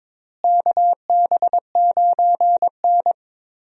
Morse code, heard as a keyed tone, about 22 wpm, 700 Hz. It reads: KB9N
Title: USA Call Signs - 4 Characters - Repeated 2x - 22wpm